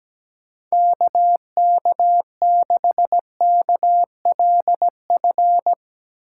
Morse code, heard as a keyed tone, about 17 wpm, 700 Hz. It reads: KK6KLF
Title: USA Call Signs - 6 Characters - 3x Character Spacing - 17wpm